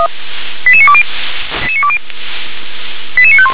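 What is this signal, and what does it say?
Five-tone selective call (ZVEI), heard as quick runs of five tones, tones of about 70 ms. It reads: 88920 88951
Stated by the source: ZVEI five-tone selective call received by SDR.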